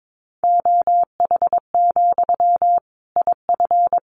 Morse code, heard as a keyed tone, about 22 wpm, 700 Hz. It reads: OH, IF
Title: Sentences from Top 500 Words 22wpm